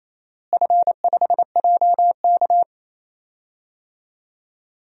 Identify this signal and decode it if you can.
Morse code, heard as a keyed tone, about 28 wpm, 700 Hz. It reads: F5JK
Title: DXCC-100 Call Signs - 4 Characters - Repeated 3x - 28wpm